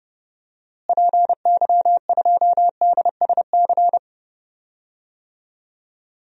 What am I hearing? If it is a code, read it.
Morse code, heard as a keyed tone, about 30 wpm, 700 Hz. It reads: PY2DSC
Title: DXCC-100 Call Signs - 6 Characters - Repeated 2x - 30wpm